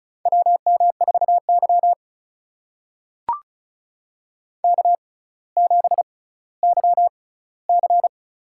Morse code, heard as a keyed tone, about 36 wpm, spaced as for 18 wpm, 700 Hz. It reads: WM4Y K7YC
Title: USA Call Signs - 4 Characters - 6x Character Spacing - 35wpm